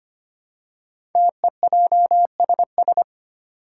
Morse code, heard as a keyed tone, about 25 wpm, 700 Hz. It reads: TEJSS
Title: Code Groups - 5 Letters - 25wpm